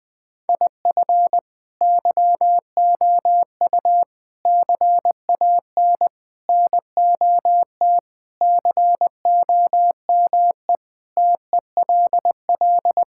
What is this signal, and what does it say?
Morse code, heard as a keyed tone, about 19 wpm, 700 Hz. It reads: IF YOU CAN NOT COME TELL
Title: Sentences with 4-Word Phrases - 20wpm